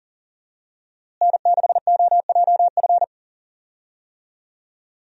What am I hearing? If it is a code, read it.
Morse code, heard as a keyed tone, about 40 wpm, 700 Hz. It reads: N6OJF
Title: USA Call Signs - 5 Characters - Repeated 2x - 40wpm